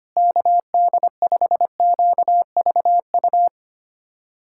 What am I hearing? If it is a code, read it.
Morse code, heard as a keyed tone, about 25 wpm, 700 Hz. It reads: KD5QVU
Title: USA Callsign DE Callsign - 6 Characters - 25wpm